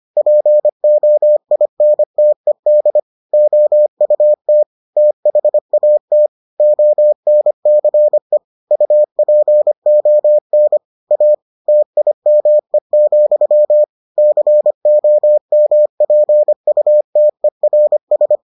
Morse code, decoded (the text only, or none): POINTED OUT THAT ONCE UPON A TIME, COMPUTERS